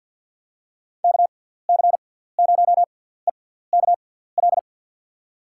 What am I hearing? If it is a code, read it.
Morse code, heard as a keyed tone, about 48 wpm, spaced as for 17 wpm, 700 Hz. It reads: KC0EKL
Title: USA Call Signs - 6 Characters - 6x Character Spacing - 50wpm